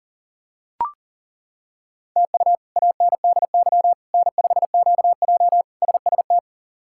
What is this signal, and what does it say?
Morse code, heard as a keyed tone, about 37 wpm, 690 Hz. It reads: TU ANDY N5QJ SST
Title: K1USN SST QSO Practice - 40wpm